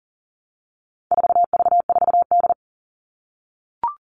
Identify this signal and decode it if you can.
Morse code, heard as a keyed tone, about 40 wpm, 700 Hz. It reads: 4V4D
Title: All Cut Numbers - 4-Digit 40wpm